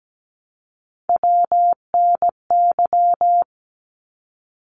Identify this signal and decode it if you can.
Morse code, heard as a keyed tone, about 17 wpm, 700 Hz. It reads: WNY
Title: ARRL Sections 17wpm